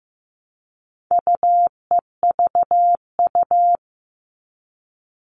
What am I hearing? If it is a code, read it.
Morse code, heard as a keyed tone, about 15 wpm, 700 Hz. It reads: UEVU
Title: All Cut Numbers - 4-Digit 15wpm